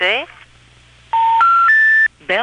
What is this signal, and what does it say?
Telephone call progress tone, heard as a telephone network's tone, with speech. Special information tone.